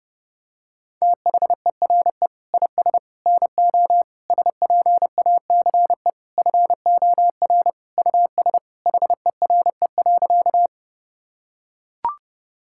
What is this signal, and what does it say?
Morse code, heard as a keyed tone, about 32 wpm, 700 Hz. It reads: THERE IS NO SPACE FOR US HERE.
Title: Sentences from Top 1000 Words - Review Part 5 - 30wpm